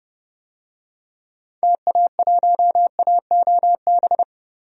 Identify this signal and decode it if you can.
Morse code, heard as a keyed tone, about 29 wpm, 700 Hz. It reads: TA1AOB
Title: DXCC-100 Call Signs - 6 Characters - Repeated 2x - 30wpm